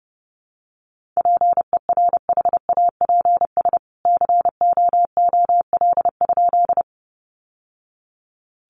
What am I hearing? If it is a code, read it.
Morse code, heard as a keyed tone, about 30 wpm, 700 Hz. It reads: PERHAPS COOL?
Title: Sentences from Top 1200 Words 30wpm